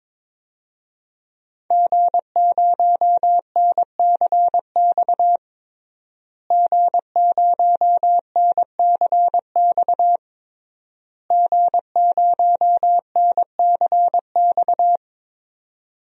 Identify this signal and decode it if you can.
Morse code, heard as a keyed tone, about 22 wpm, 700 Hz. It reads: G0NCX G0NCX G0NCX
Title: DXCC-100 Call Signs - 5 Characters - Repeated 3x - 22wpm